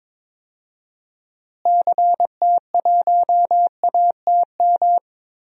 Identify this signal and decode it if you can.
Morse code, heard as a keyed tone, about 21 wpm, 700 Hz. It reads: CT1ATM
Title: DXCC-100 Call Signs - 6 Characters - Repeated 2x - 22wpm